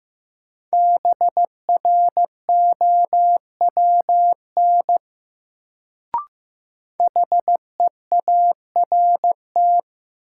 Morse code, heard as a keyed tone, about 15 wpm, 700 Hz. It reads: BROWN HEART
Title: Sentences from Top 1200 Words - New Word Drill Alt - 15wpm